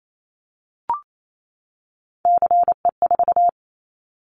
Morse code, heard as a keyed tone, about 28 wpm, 700 Hz. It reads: CE4
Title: Callsign Prefixes for Top 100 DXCC - 28wpm